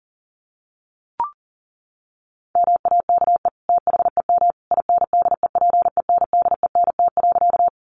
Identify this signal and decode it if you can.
Morse code, heard as a keyed tone, about 40 wpm, 700 Hz. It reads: MAKE THEM INDEPENDENT.
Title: Sentences from Top 900 Words 40wpm